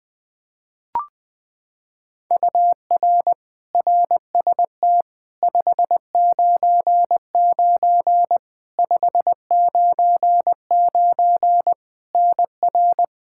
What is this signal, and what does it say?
Morse code, heard as a keyed tone, about 20 wpm, 700 Hz. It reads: UR RST 599 599 NR
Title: Practice QSO - Exchange 1 - Beginning - 20wpm